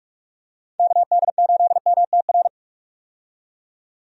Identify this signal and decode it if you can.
Morse code, heard as a keyed tone, about 42 wpm, 700 Hz. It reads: KD8MTR
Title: USA Call Signs - 6 Characters - 45wpm